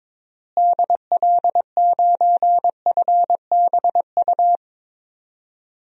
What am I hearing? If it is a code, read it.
Morse code, heard as a keyed tone, about 22 wpm, 700 Hz. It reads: DL9FBU
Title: DXCC-100 Call Signs - 6 Characters - Repeated 3x - 22wpm